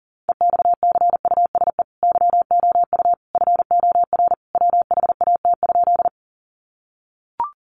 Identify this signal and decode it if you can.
Morse code, heard as a keyed tone, about 40 wpm, 700 Hz. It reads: EXCUSE YOU FOR WHAT?
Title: Sentences from Top 700 Words 40wpm